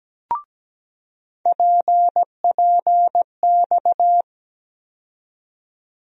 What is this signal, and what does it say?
Morse code, heard as a keyed tone, about 17 wpm, 700 Hz. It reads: PPX
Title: USA Call Sign Postfixes - 3 Letters 17wpm